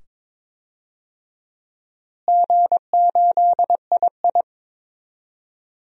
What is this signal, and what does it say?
Morse code, heard as a keyed tone, about 22 wpm, 700 Hz. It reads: G8II
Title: DXCC-100 Call Signs - 4 Characters - Repeated 2x - 22wpm